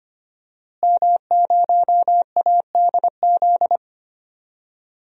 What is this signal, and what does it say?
Morse code, heard as a keyed tone, about 25 wpm, 700 Hz. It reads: M0ADZ
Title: DXCC-100 Call Signs - 5 Characters - Repeated 3x - 25wpm